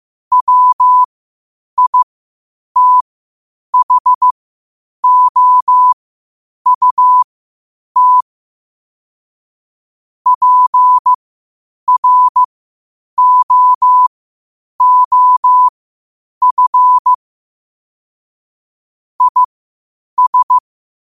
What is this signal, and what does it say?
Morse code, heard as a keyed tone, about 15 wpm, spaced as for 8 wpm, 1000 Hz. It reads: WITHOUT PROOF IS
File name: qotdfiles / 8wpm